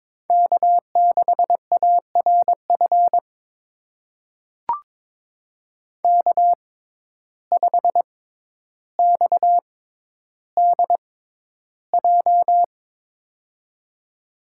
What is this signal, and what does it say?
Morse code, heard as a keyed tone, about 22 wpm, spaced as for 11 wpm, 700 Hz. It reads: K6ARF K5XDJ
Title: USA Call Signs - 6x Character Spacing - 22wpm